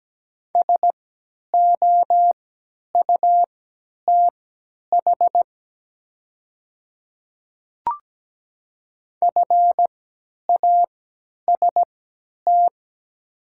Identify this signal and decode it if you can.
Morse code, heard as a keyed tone, about 17 wpm, spaced as for 10 wpm, 700 Hz. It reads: SOUTH FAST
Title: 3x Character Spacing - Top 500 Words - 17wpm